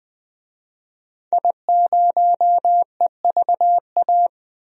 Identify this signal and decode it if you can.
Morse code, heard as a keyed tone, about 20 wpm, 700 Hz. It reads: I0EVA
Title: DXCC-100 Call Signs - 5 Characters - 20wpm